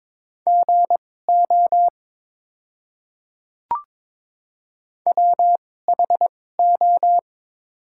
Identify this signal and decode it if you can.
Morse code, heard as a keyed tone, about 22 wpm, spaced as for 16 wpm, 700 Hz. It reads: GO WHO
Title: 2x Character Spacing - Top 200 Words - 22wpm